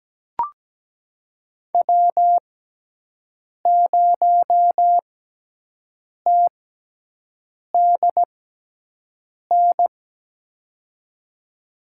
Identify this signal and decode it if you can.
Morse code, heard as a keyed tone, about 17 wpm, spaced as for 6 wpm, 700 Hz. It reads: W0TDN
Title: USA Call Signs - 6x Character Spacing - 17wpm